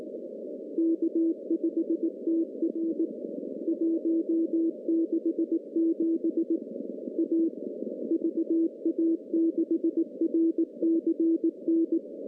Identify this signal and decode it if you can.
Morse code, heard as a keyed tone, about 18 wpm, 330 Hz. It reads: K5TR 167 A VA6RCN
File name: contests / CW-SS-k5tr- -0700